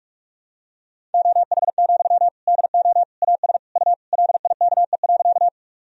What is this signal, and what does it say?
Morse code, heard as a keyed tone, about 41 wpm, 690 Hz. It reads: OH, DO AS U LIKE.
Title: Sentences from Top 500 Words and Abbreviations 45wpm